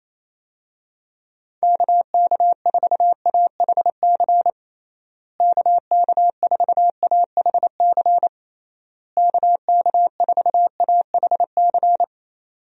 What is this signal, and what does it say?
Morse code, heard as a keyed tone, about 28 wpm, 700 Hz. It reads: KK4AHC KK4AHC KK4AHC
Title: USA Call Signs - 6 Characters - Repeated 3x - 28wpm